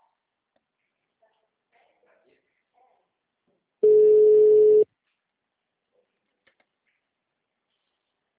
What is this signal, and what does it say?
Ringback tone (Europe, 425 Hz).
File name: out- -2013- - - .2056